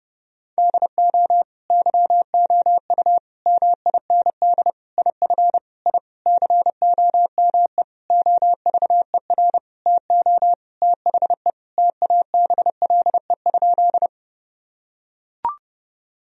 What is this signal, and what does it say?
Morse code, heard as a keyed tone, about 29 wpm, 700 Hz. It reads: DO YOU MIND IF I COME OVER TO THE TABLE?
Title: Sentences from Top 1000 Words - Review Part 2 - 30wpm